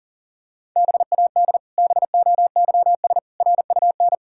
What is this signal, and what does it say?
Morse code, heard as a keyed tone, about 40 wpm, 700 Hz. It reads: BAD BOYS RUN
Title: Pangram Challenge v2 - 40wpm